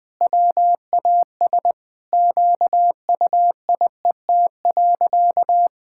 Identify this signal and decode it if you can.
Morse code, heard as a keyed tone, about 20 wpm, 700 Hz. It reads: WAS QUIET.
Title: Sentences with 5-Word Phrases - 20wpm